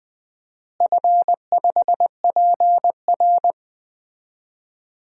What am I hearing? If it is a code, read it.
Morse code, heard as a keyed tone, about 20 wpm, 700 Hz. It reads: F5PR
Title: DXCC-100 Call Signs - 4 Characters - Repeated 3x - 20wpm